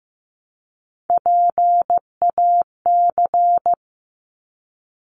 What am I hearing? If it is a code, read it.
Morse code, heard as a keyed tone, about 15 wpm, 700 Hz. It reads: PAC
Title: ARRL Sections 15wpm